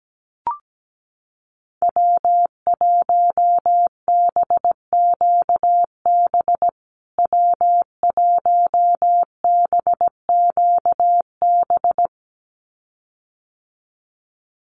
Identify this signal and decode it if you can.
Morse code, heard as a keyed tone, about 17 wpm, 700 Hz. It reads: W1BQB W1BQB
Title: USA Call Signs Repeated 2x 17wpm